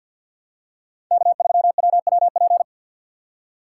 Morse code, heard as a keyed tone, about 53 wpm, 700 Hz. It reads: K3WWP
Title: K1USN SST QSO Practice - 50wpm